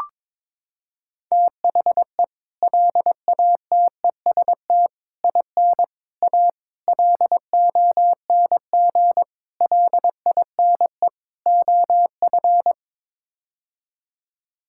Morse code, heard as a keyed tone, about 22 wpm, 700 Hz. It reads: THE LATEST IN A LONG LINE OF
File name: practice / 22wpm / Sets of 7 Words - 22wpm